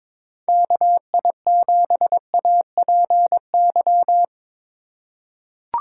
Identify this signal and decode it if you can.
Morse code, heard as a keyed tone, about 21 wpm, 700 Hz. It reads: KI7APY
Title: USA Call Signs - 4x Character Spacing - 22wpm